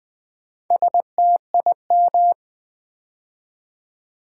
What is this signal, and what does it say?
Morse code, heard as a keyed tone, about 20 wpm, 700 Hz. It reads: STIM
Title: Build A Word - 4 Letters At A Time - 8-Letter Words - 20wpm